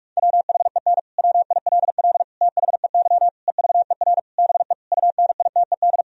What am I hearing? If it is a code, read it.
Morse code, heard as a keyed tone, about 46 wpm, 700 Hz. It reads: WHEN WILL THEY EVER BE UNITED
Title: Sentences from Top 1000 Words - Review Part 4 - 45wpm